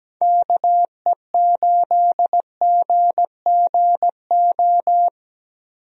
Morse code, heard as a keyed tone, about 17 wpm, 700 Hz. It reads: KE8GGO
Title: USA Call Signs - 6 Characters - Repeated 2x - 17wpm